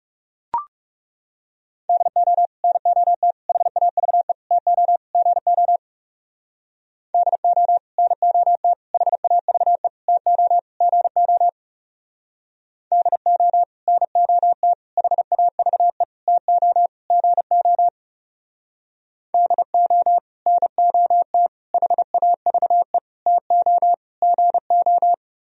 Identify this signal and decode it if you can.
Morse code, heard as a keyed tone, about 37 wpm, 700 Hz. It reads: DO NOT HAVE TO GO DO NOT HAVE TO GO DO NOT HAVE TO GO DO NOT HAVE TO GO
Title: Speed-Racing - Sets of 5 Words 30wpm